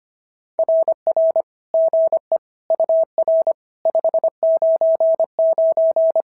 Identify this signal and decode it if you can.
Morse code, heard as a keyed tone, about 25 wpm, 620 Hz. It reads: RR GE UR 599
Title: SOTA QSO Practice 25wpm